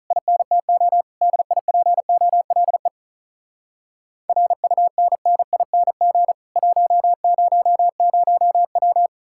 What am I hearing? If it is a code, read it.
Morse code, heard as a keyed tone, about 46 wpm, 700 Hz. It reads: INTO DIPOLE RUNNING 100W